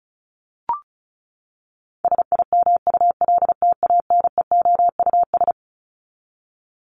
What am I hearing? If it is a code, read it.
Morse code, heard as a keyed tone, about 34 wpm, 700 Hz. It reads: SIMULTANEOUS
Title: LY Suffix - Top 100 Words with Base Words 35wpm